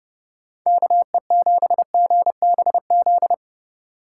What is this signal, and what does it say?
Morse code, heard as a keyed tone, about 30 wpm, 700 Hz. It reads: KE7GBZ